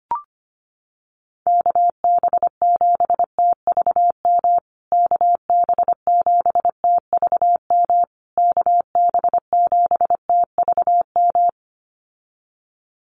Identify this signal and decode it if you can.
Morse code, heard as a keyed tone, about 25 wpm, 700 Hz. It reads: KB7TVM KB7TVM KB7TVM
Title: USA Call Signs Repeated 3x 25wpm